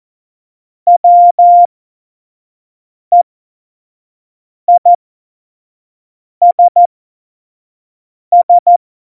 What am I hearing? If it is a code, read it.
Morse code, heard as a keyed tone, about 14 wpm, spaced as for 5 wpm, 700 Hz. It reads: WEISS
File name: Morsekurs / Witze / Witz 11. Juni 2019, 25 F70